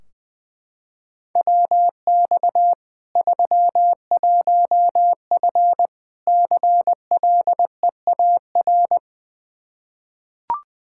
Morse code, heard as a keyed tone, about 21 wpm, 700 Hz. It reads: WX 31F CLEAR
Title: Practice QSO - Exchange 2 - Info - 20wpm